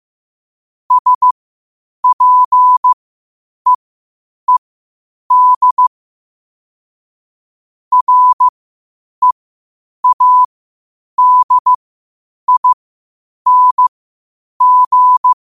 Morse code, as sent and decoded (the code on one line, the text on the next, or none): ... .--. . -.. / .-. .- -.. .. -. --.
SPEED READING